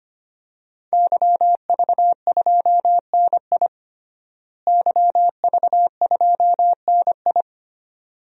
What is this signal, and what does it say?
Morse code, heard as a keyed tone, about 25 wpm, 700 Hz. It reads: YV2NI YV2NI
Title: DXCC-100 Call Signs - 5 Characters - Repeated 2x - 25wpm